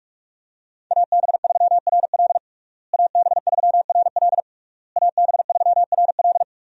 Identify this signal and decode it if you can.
Morse code, heard as a keyed tone, about 46 wpm, 700 Hz. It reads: AB3RL AB3RL AB3RL
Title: USA Call Signs - 5 Characters - Repeated 3x - 45wpm